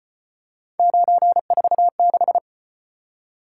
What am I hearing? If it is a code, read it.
Morse code, heard as a keyed tone, about 35 wpm, 700 Hz. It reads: 946